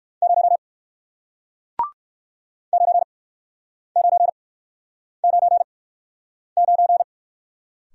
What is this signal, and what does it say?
Morse code, heard as a keyed tone, about 61 wpm, spaced as for 9 wpm, 700 Hz. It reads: 0 9999